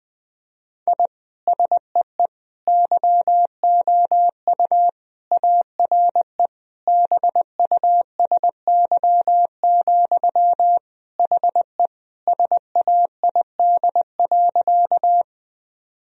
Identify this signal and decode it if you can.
Morse code, heard as a keyed tone, about 20 wpm, 700 Hz. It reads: I SEE YOU ARE BUSY, HE SAID.
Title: Sentences from Top 1000 Words - Review Part 3 - 20wpm